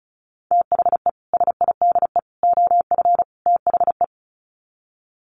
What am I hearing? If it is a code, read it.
Morse code, heard as a keyed tone, about 34 wpm, 700 Hz. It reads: THE SIDE OF THE
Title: Sets of 4 Words 35wpm